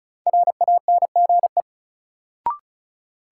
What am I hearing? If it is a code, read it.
Morse code, heard as a keyed tone, about 36 wpm, 700 Hz. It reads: RANGE
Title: Sentences from Top 900 Words - New Word Drill - 35wpm